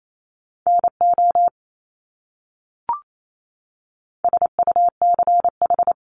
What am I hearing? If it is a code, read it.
Morse code, heard as a keyed tone, about 28 wpm, 700 Hz. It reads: NO SUCH
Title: Top 200 Words Repeated 28wpm